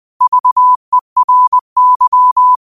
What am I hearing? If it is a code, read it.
Morse code, heard as a keyed tone, about 20 wpm, 1000 Hz. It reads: VERY
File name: qotdfiles / 20wpm